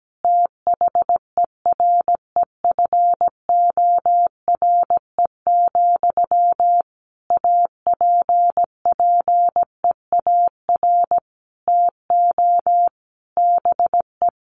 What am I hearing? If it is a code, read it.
Morse code, heard as a keyed tone, about 17 wpm, 700 Hz. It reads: THEREFORE, APPEAR TO BE
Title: Sentences from Top 1200 Words 17wpm